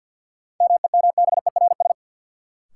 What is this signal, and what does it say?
Morse code, heard as a keyed tone, about 52 wpm, 700 Hz. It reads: MEMBERS
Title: Sets of 3 Words - New Word Drill - 50wpm